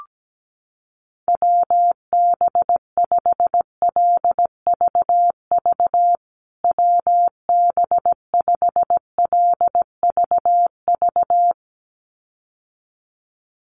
Morse code, heard as a keyed tone, about 17 wpm, 700 Hz. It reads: WB5LVV WB5LVV
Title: USA Call Signs Repeated 2x 17wpm